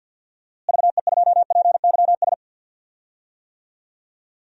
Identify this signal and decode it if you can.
Morse code, heard as a keyed tone, about 52 wpm, 700 Hz. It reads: VE2PYS
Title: DXCC-100 Call Signs - 6 Characters - Repeated 3x - 50wpm